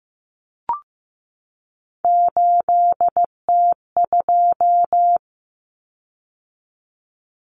Morse code, heard as a keyed tone, about 15 wpm, 700 Hz. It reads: 8T2
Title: Common Cut Numbers - 3-Digit 15wpm